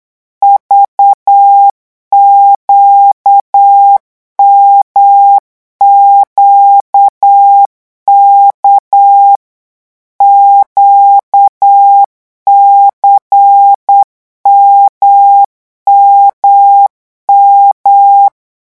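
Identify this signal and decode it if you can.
Morse code, heard as a keyed tone, about 8 wpm, 800 Hz. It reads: VQMQK QCMMM
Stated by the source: Morse code revision lesson at 8 words per minute, from a French amateur radio course.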